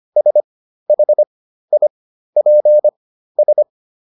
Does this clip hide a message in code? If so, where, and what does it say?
Yes: Morse code from 0.2 to 3.6 s, heard as a keyed tone, about 25 wpm, spaced as for 13 wpm, 600 Hz. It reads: SHIPS